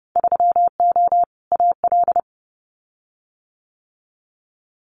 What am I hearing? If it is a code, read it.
Morse code, heard as a keyed tone, about 30 wpm, 700 Hz. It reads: 3O AL
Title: Winter Field Day Exchanges Encore 30wpm